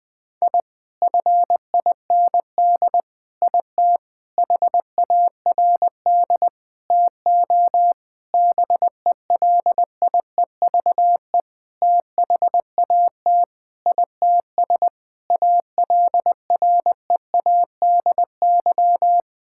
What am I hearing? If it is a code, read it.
Morse code, heard as a keyed tone, about 21 wpm, 700 Hz. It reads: I FIND IT HARD TO BELIEVE THAT ITS ALREADY
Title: Sentences with 7-Word Phrases - 20wpm